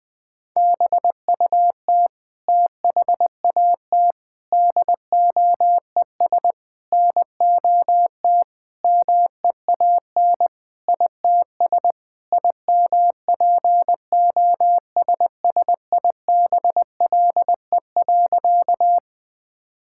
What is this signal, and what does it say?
Morse code, heard as a keyed tone, about 21 wpm, 700 Hz. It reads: BUT THAT DOES NOT MEAN ITS IMPOSSIBLE.
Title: Sentences with 5-Word Phrases - Spoken Beginning - 20wpm